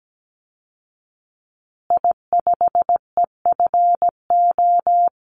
Morse code, heard as a keyed tone, about 17 wpm, 700 Hz. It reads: I5EFO
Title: Call Signs Repeated 2x - 17wpm